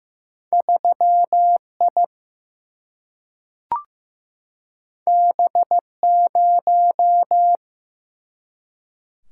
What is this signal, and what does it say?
Morse code, heard as a keyed tone, about 15 wpm, 700 Hz. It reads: 3I B0
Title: Single-Letter Single-Number 15wpm